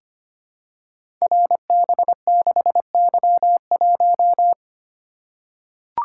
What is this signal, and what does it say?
Morse code, heard as a keyed tone, about 25 wpm, 700 Hz. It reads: RB6Y1
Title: Code Groups - 5 Letter:Numbers - 25wpm